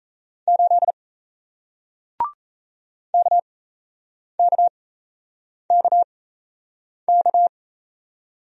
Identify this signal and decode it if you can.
Morse code, heard as a keyed tone, about 38 wpm, spaced as for 8 wpm, 700 Hz. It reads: 8 KKKK